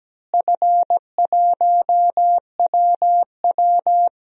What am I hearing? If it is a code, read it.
Morse code, heard as a keyed tone, about 17 wpm, 700 Hz. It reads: F1WW
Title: DXCC-100 Call Signs - 4 Characters - Repeated 3x - 17wpm